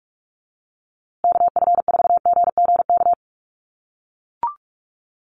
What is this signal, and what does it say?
Morse code, heard as a keyed tone, about 44 wpm, 700 Hz. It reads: KF4GGK
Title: USA Call Signs Repeated 2x 45wpm